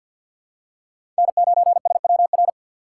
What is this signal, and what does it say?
Morse code, heard as a keyed tone, about 52 wpm, 700 Hz. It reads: N9SWR
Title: USA Call Signs - 5 Characters - 6x Character Spacing - 50wpm